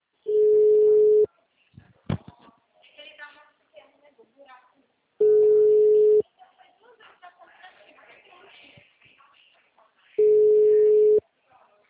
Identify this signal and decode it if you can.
Ringback tone (Europe, 425 Hz), with speech.